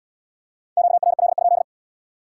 Morse code, heard as a keyed tone, about 78 wpm, 700 Hz. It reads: QRL?